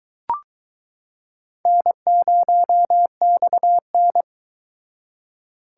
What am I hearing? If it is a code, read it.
Morse code, heard as a keyed tone, about 22 wpm, 700 Hz. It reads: N0XN